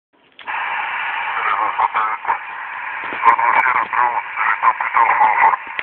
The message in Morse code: none